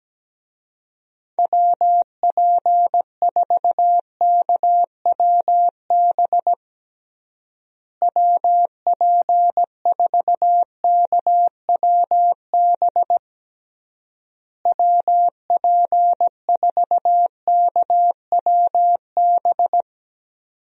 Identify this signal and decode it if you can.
Morse code, heard as a keyed tone, about 17 wpm, 700 Hz. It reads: WP4KWB WP4KWB WP4KWB
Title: USA Call Signs - 6 Characters - Repeated 3x - 17wpm